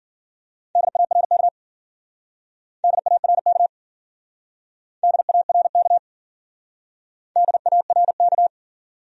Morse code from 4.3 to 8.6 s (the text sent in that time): DARK DARK